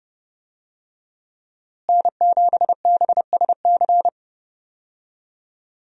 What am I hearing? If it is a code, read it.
Morse code, heard as a keyed tone, about 32 wpm, 700 Hz. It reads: N7BSC
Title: USA Call Signs - 5 Characters - Repeated 2x - 30wpm